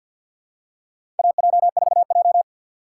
Morse code, heard as a keyed tone, about 51 wpm, 700 Hz. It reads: AJ4J
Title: USA Call Signs - 4 Characters - 3x Character Spacing - 50wpm